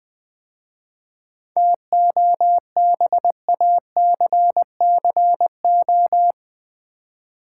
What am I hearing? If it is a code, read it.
Morse code, heard as a keyed tone, about 20 wpm, 700 Hz. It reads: TOBACCO
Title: 7 Letter Words - First 3 Letters Given - 20wpm